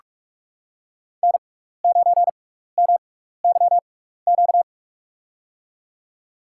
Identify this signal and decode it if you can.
Morse code, heard as a keyed tone, about 43 wpm, spaced as for 15 wpm, 700 Hz. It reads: N9MYQ
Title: USA Call Signs - 5 Characters - 6x Character Spacing - 45wpm